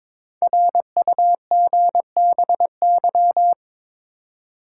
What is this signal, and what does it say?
Morse code, heard as a keyed tone, about 22 wpm, 700 Hz. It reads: RUGBY
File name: 5 Letter Words - First 3 Letters Given - 22wpm